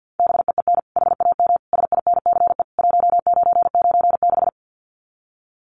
Morse code, heard as a keyed tone, about 50 wpm, 700 Hz. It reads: BEEN HAM SINCE 1996
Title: Common QSO Phrases 50wpm